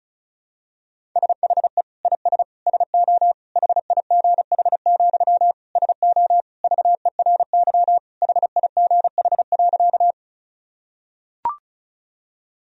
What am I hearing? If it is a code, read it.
Morse code, heard as a keyed tone, about 36 wpm, 690 Hz. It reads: SHE IS SO HIGH, SO VERY HIGH.
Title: Sentences from Top 1000 Words - Review Part 1 - 35wpm